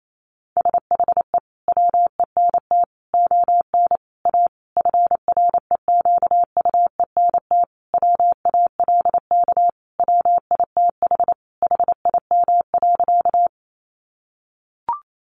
Morse code, heard as a keyed tone, about 28 wpm, 700 Hz. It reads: SHE WENT ON A FREQUENT WALK WITH HIM.
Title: Sentences from Top 1400 Words 28wpm